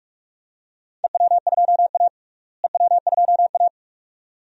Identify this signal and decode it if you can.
Morse code, heard as a keyed tone, about 47 wpm, 700 Hz. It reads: EW2A EW2A
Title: DXCC-100 Call Signs - 4 Characters - Repeated 2x - 45wpm